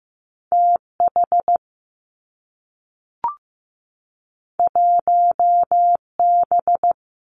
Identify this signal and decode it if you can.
Morse code, heard as a keyed tone, about 15 wpm, 700 Hz. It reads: TH 1B
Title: Two Challenging Letter-Number Combinations 15wpm